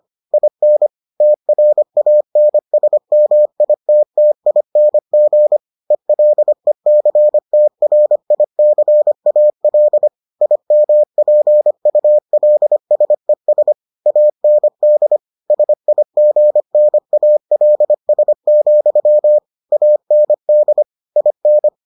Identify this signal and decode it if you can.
Morse code, heard as a keyed tone, about 25 wpm, 600 Hz. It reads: IN TRANSMITTING ELECTRICAL IMPULSES AND SIGNALS, AND IN